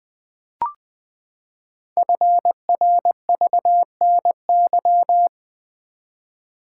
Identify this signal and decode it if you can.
Morse code, heard as a keyed tone, about 20 wpm, 700 Hz. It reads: FRVNY
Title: Code Groups - 5 Letters - 20wpm